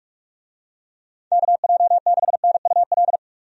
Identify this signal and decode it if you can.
Morse code, heard as a keyed tone, about 44 wpm, 700 Hz. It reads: KJ6NUL
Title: USA Call Signs - 6 Characters - 45wpm